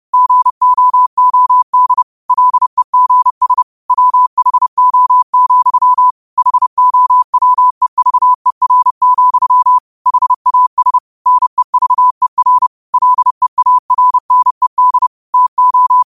Morse code, heard as a keyed tone, about 30 wpm, 1000 Hz. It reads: GOOD LEGS WHO, HOWEVER, HAS NEVER LEARNED TO
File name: qotdfiles / 30wpm